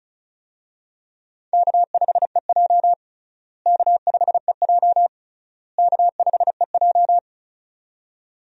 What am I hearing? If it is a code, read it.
Morse code, heard as a keyed tone, about 36 wpm, 700 Hz. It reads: K5EJ K5EJ K5EJ
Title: USA Call Signs - 4 Characters - Repeated 3x - 35wpm